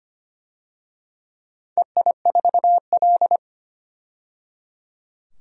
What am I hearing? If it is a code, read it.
Morse code, heard as a keyed tone, about 25 wpm, 700 Hz. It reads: EI4L
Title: DXCC-100 Call Signs - 4 Characters - 25wpm